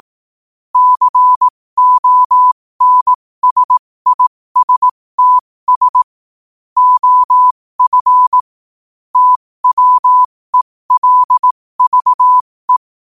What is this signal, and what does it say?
Morse code, heard as a keyed tone, about 18 wpm, spaced as for 15 wpm, 1000 Hz. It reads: CONSISTS OF TWELVE
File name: qotdfiles / 15wpm